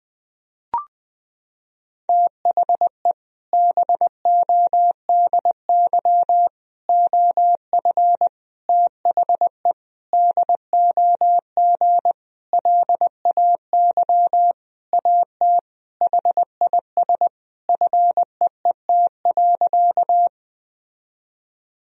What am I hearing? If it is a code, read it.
Morse code, heard as a keyed tone, about 20 wpm, 700 Hz. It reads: THE BODY OF THE DOG LAY AT HIS FEET.
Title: Sentences from Top 1000 Words - Review Part 2 - 20wpm